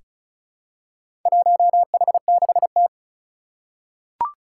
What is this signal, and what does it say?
Morse code, heard as a keyed tone, about 36 wpm, 700 Hz. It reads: 1H6T
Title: Four Challenging Letter-Number Combinations 35wpm